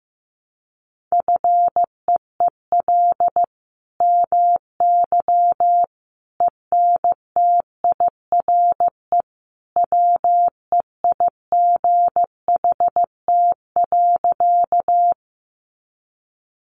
Morse code, heard as a keyed tone, about 15 wpm, 700 Hz. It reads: FEEL MY ENTIRE WEIGHT.
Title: Sentences from Top 800 Words 15wpm